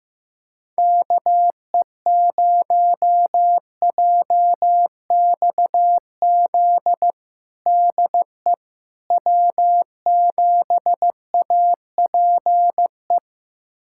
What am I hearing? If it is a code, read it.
Morse code, heard as a keyed tone, about 15 wpm, 700 Hz. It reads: KE0JXZ DE W7APE
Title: USA Callsign DE Callsign 15wpm